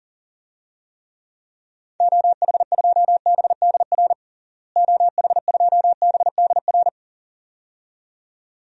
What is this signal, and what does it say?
Morse code, heard as a keyed tone, about 40 wpm, 700 Hz. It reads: OH2BDR OH2BDR
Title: DXCC-100 Call Signs - 6 Characters - Repeated 2x - 40wpm